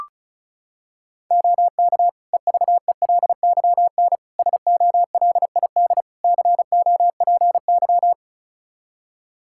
Morse code, heard as a keyed tone, about 36 wpm, 700 Hz. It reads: OK EVELYN SOLID COPY
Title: Practice QSO - Exchange 3 - More Info - 35wpm